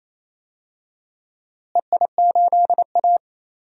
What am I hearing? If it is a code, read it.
Morse code, heard as a keyed tone, about 28 wpm, 700 Hz. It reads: EI8A
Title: DXCC-100 Call Signs - 4 Characters - Repeated 2x - 28wpm